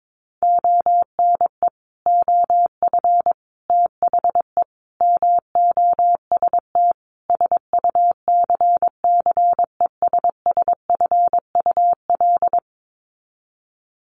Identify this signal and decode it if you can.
Morse code, heard as a keyed tone, about 22 wpm, 700 Hz. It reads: ONE OF THE MOST SUCCESSFUL
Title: Sets of 5 Words Encore2 22wpm